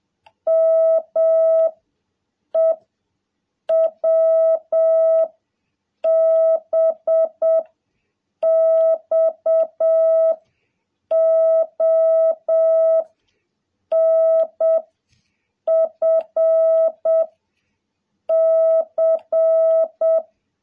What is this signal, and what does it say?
Morse code, heard as a keyed tone, about 7 wpm, 660 Hz. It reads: M E W B X O N F C
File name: jh8cbh / 2021cw